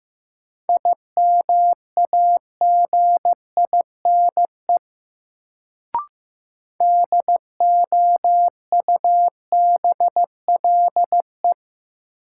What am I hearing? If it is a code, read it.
Morse code, heard as a keyed tone, about 15 wpm, 700 Hz. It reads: IMAGINE DOUBLE
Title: Sentences from Top 1400 Words - New Word Drill - 15wpm